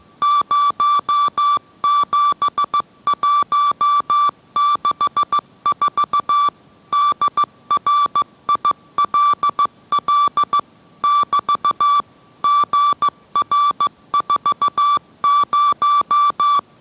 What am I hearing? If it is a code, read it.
Morse code, heard as a keyed tone, about 15 wpm, 1160 Hz. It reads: 07164 DRILL <BT> GR40